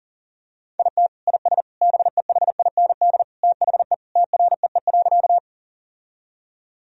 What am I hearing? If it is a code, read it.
Morse code, heard as a keyed tone, about 40 wpm, 700 Hz. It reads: IT IS BEHIND THE TREE.
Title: Sentences from Top 1000 Words - Review Part 1 - 40wpm